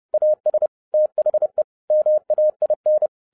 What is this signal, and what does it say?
Morse code, heard as a keyed tone, about 30 wpm, 610 Hz. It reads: AS THE MAIN